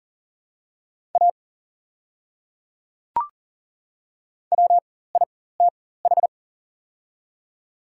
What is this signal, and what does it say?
Morse code, heard as a keyed tone, about 40 wpm, spaced as for 19 wpm, 700 Hz. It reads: A WITH
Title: 4x Character Spacing - Top 200 Words - 40wpm